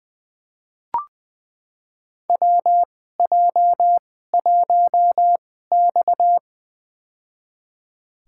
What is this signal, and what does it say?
Morse code, heard as a keyed tone, about 21 wpm, spaced as for 15 wpm, 700 Hz. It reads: WJ1X
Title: USA Call Signs - 4 Characters - 2x Character Spacing - 20wpm